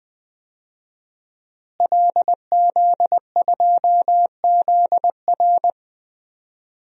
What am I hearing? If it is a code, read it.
Morse code, heard as a keyed tone, about 20 wpm, 700 Hz. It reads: LZ2ZR
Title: DXCC-100 Call Signs - 5 Characters - Repeated 3x - 20wpm